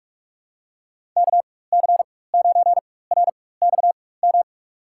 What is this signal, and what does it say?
Morse code, heard as a keyed tone, about 45 wpm, spaced as for 21 wpm, 700 Hz. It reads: KC9RXM
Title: USA Call Signs - 6 Characters - 4x Character Spacing - 45wpm